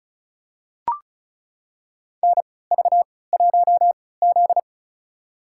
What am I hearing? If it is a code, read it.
Morse code, heard as a keyed tone, about 34 wpm, spaced as for 20 wpm, 700 Hz. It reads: NV1Z
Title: USA Call Signs - 4 Characters - 3x Character Spacing - 35wpm